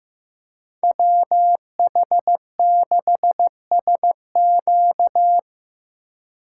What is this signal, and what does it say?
Morse code, heard as a keyed tone, about 18 wpm, 700 Hz. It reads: WH6SQ
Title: Speed Racing - USA Call Signs 15wpm